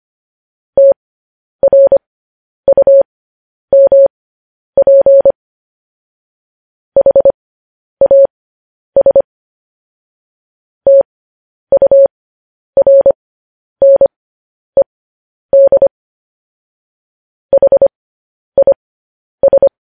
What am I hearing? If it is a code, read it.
Morse code, heard as a keyed tone, about 25 wpm, spaced as for 10 wpm, 550 Hz. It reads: TRUMP HAS TURNED HIS
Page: news headlines in morse code at 10 wpm